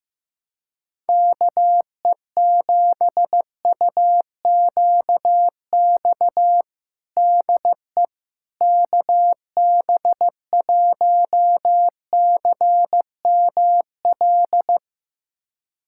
Morse code, heard as a keyed tone, about 15 wpm, 700 Hz. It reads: KE7UQX DE KB1CML
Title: USA Callsign DE Callsign 15wpm